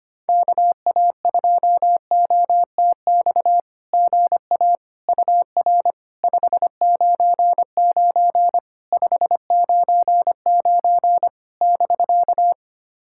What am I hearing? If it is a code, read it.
Morse code, heard as a keyed tone, about 25 wpm, 700 Hz. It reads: KA2OTX GA UR 599 599 <BK>
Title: SOTA QSO Practice 25wpm